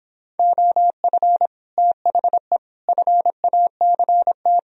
Morse code, heard as a keyed tone, about 26 wpm, 700 Hz. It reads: OF THE FACT